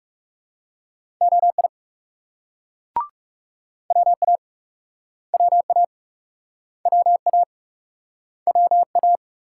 Morse code, heard as a keyed tone, about 42 wpm, 700 Hz. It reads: OI WA WA WA WA